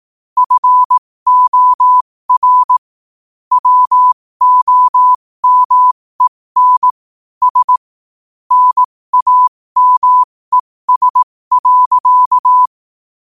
Morse code, heard as a keyed tone, about 18 wpm, spaced as for 15 wpm, 1000 Hz. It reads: FOR WOMEN S NAMES.
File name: qotdfiles / 15wpm